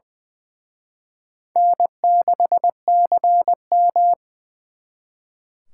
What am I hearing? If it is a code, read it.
Morse code, heard as a keyed tone, about 24 wpm, 700 Hz. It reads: N6CM
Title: Speed Racing - USA 1x2 Call Signs 20wpm